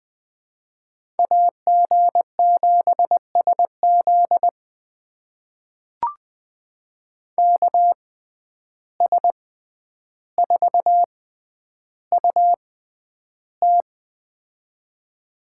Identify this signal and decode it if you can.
Morse code, heard as a keyed tone, about 20 wpm, spaced as for 10 wpm, 700 Hz. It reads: AG7SZ KS4UT
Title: USA Call Signs - 5 Characters - 6x Character Spacing - 20wpm